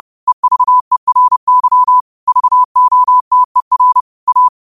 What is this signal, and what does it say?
Morse code, heard as a keyed tone, about 30 wpm, 1000 Hz. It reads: EVERY VOTER A
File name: qotdfiles / 30wpm